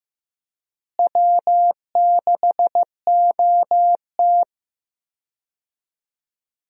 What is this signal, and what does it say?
Morse code, heard as a keyed tone, about 15 wpm, 700 Hz. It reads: W6OT
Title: USA Call Signs - 4 Characters - Repeated 3x - 15wpm